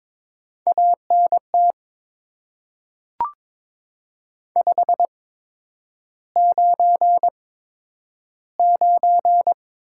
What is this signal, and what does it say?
Morse code, heard as a keyed tone, about 22 wpm, spaced as for 9 wpm, 700 Hz. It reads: ANT 599
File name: Top 90 QSO Elements - 8x Character Spacing - 22wpm